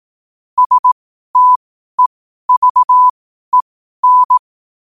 Morse code, heard as a keyed tone, about 18 wpm, spaced as for 12 wpm, 1000 Hz. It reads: STEVEN
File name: qotdfiles / 12wpm